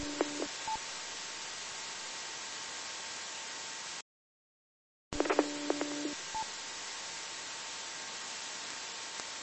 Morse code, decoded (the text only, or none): none